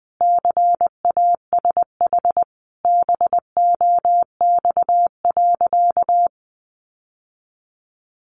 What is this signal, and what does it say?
Morse code, heard as a keyed tone, about 20 wpm, 700 Hz. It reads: CASH BOX.
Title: Sentences from Top 700 Words 20wpm